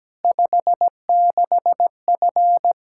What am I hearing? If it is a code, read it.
Morse code, heard as a keyed tone, about 17 wpm, 700 Hz. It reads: 56F
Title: Practice QSO - Exchange 2 - Info - 17wpm